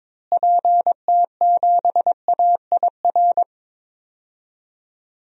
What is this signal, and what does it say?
Morse code, heard as a keyed tone, about 22 wpm, 700 Hz. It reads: PT7AIR
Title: DXCC-100 Call Signs - 6 Characters - Repeated 2x - 22wpm